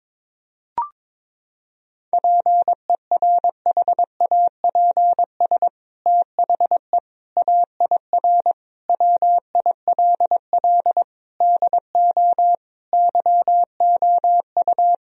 Morse code, heard as a keyed tone, about 22 wpm, 700 Hz. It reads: PERHAPS THE AIR WILL DO YOU
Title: Sentences from Top 1000 Words - Review Part 1 - 22wpm